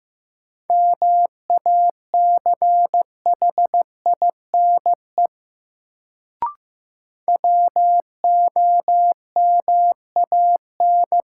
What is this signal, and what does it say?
Morse code, heard as a keyed tone, about 15 wpm, 700 Hz. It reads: MACHINE WOMAN
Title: Sentences from Top 1000 Words - Review Part 2 - New Word Drill - 15wpm